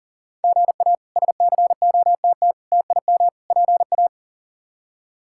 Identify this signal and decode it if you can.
Morse code, heard as a keyed tone, about 40 wpm, 700 Hz. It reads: GA SCOTT TIM PA
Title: K1USN SST QSO Practice - 40wpm